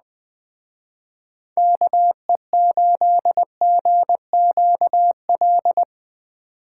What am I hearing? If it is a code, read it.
Morse code, heard as a keyed tone, about 21 wpm, 700 Hz. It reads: KE8GQL
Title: Speed Racing - USA Call Signs 17wpm